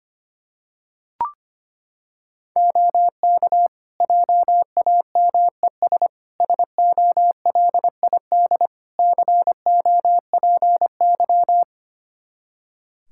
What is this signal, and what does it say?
Morse code, heard as a keyed tone, about 25 wpm, 700 Hz. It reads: OK JAMES SOLID COPY
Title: Practice QSO - Exchange 3 - More Info - 25wpm